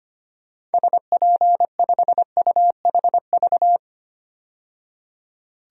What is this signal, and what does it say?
Morse code, heard as a keyed tone, about 25 wpm, 700 Hz. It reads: SP5UHV
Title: DXCC-100 Call Signs - 6 Characters - 25wpm